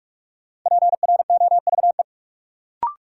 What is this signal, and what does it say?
Morse code, heard as a keyed tone, about 47 wpm, 700 Hz. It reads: PROVE
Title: Sentences from Top 1600 Words - New Word Drill - 45wpm